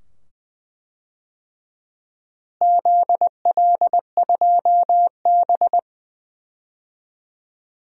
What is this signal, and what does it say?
Morse code, heard as a keyed tone, about 20 wpm, 700 Hz. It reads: ZL2B
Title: DXCC-100 Call Signs - 4 Characters - Repeated 2x - 20wpm